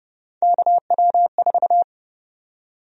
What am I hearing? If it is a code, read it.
Morse code, heard as a keyed tone, about 30 wpm, 700 Hz. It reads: KW4
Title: USA Call Sign Prefixes - 30wpm